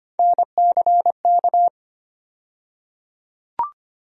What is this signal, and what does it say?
Morse code, heard as a keyed tone, about 25 wpm, 700 Hz. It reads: NCK
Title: USA Call Sign Postfixes - 3 Letters 25wpm